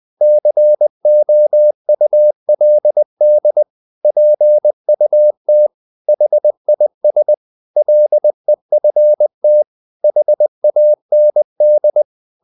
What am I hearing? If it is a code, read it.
Morse code, heard as a keyed tone, about 20 wpm, 600 Hz. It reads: COULD PUT HIS LEFT HAND
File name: the old man and the sea part 2 20wpm